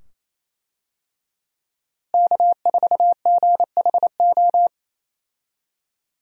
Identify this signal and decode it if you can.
Morse code, heard as a keyed tone, about 28 wpm, 700 Hz. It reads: K4GHO
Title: USA Call Signs - 5 Characters - Repeated 2x - 28wpm